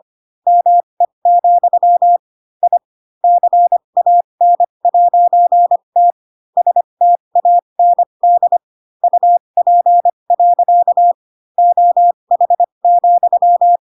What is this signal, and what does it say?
Morse code, heard as a keyed tone, about 25 wpm, spaced as for 22 wpm, 700 Hz. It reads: ME, I CAN'T STAND UP. OH,
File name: Pinocchio / Pinocchio ch7 18 25 ik1hge